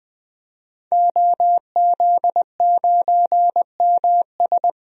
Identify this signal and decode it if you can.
Morse code, heard as a keyed tone, about 20 wpm, 700 Hz. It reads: OZ9MS
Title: DXCC-100 Call Signs - 5 Characters - 20wpm